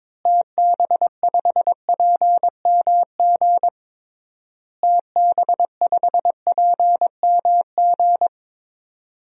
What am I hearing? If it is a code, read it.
Morse code, heard as a keyed tone, about 22 wpm, 700 Hz. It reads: TB5PMG TB5PMG
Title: DXCC-100 Call Signs - 6 Characters - Repeated 3x - 22wpm